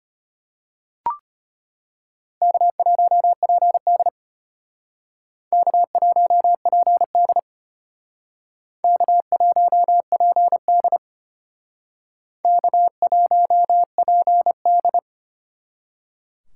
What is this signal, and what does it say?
Morse code, heard as a keyed tone, about 32 wpm, 700 Hz. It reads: K1PD K1PD K1PD K1PD